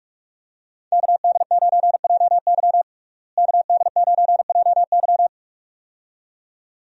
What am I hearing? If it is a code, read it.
Morse code, heard as a keyed tone, about 41 wpm, 700 Hz. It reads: KD9JY KD9JY
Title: USA Call Signs - 5 Characters - Repeated 2x - 45wpm